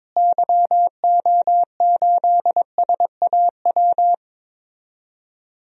Morse code, heard as a keyed tone, about 22 wpm, 700 Hz. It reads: YO8SAW
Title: DXCC-100 Call Signs - 6 Characters - Repeated 3x - 22wpm